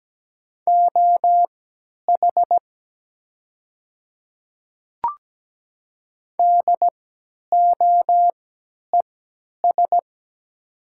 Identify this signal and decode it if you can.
Morse code, heard as a keyed tone, about 17 wpm, spaced as for 10 wpm, 700 Hz. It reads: OH DOES
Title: 3x Character Spacing - Top 500 Words - 17wpm